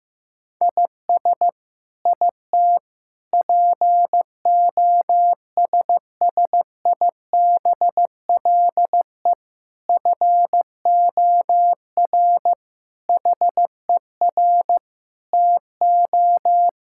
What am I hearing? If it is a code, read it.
Morse code, heard as a keyed tone, about 15 wpm, 700 Hz. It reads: IS IT POSSIBLE FOR HER TO
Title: Sentences from Top 1000 Words - Review Part 6 - 15wpm